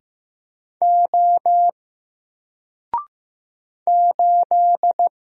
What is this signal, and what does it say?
Morse code, heard as a keyed tone, about 15 wpm, 700 Hz. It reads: O 8
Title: Single Character (Letter:Number:Punctuation:Prosign) 15wpm